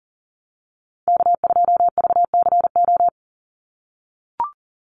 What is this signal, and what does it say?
Morse code, heard as a keyed tone, about 40 wpm, 700 Hz. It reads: K2VCO
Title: Call Signs - 40wpm